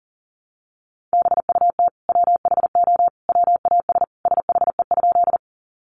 Morse code, heard as a keyed tone, about 40 wpm, 700 Hz. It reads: BUT WHO WAS SHE?
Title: Sentences from Top 100 Words 40wpm